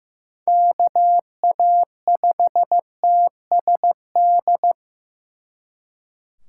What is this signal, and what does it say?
Morse code, heard as a keyed tone, about 18 wpm, 700 Hz. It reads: KA5TSD
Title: Speed Racing - USA 2x3 Call Signs 15wpm